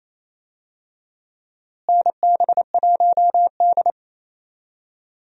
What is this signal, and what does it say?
Morse code, heard as a keyed tone, about 28 wpm, 700 Hz. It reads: NB1D
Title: USA Call Signs - 4 Characters - Repeated 3x - 28wpm